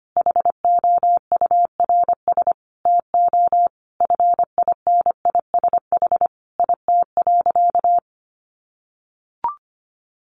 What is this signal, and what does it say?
Morse code, heard as a keyed tone, about 25 wpm, 700 Hz. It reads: HOURS TO FINISH IT.
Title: Additional Easy Phrases 25wpm